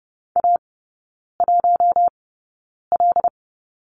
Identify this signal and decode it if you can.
Morse code, heard as a keyed tone, about 30 wpm, spaced as for 9 wpm, 700 Hz. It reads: A1L
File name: Lesson 18e - Letter m - Three Characters 30wpm